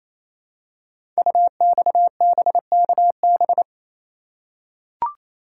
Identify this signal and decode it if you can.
Morse code, heard as a keyed tone, about 29 wpm, 700 Hz. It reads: UXBKB
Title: Code Groups - 5 Letters - 28wpm